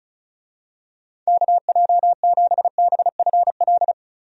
Morse code, heard as a keyed tone, about 36 wpm, 700 Hz. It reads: KJ7BFL
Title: USA Call Signs - 6 Characters - 35wpm